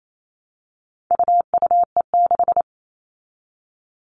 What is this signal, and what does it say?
Morse code, heard as a keyed tone, about 28 wpm, 700 Hz. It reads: UUE6
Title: All Cut Numbers - 4-Digit 28wpm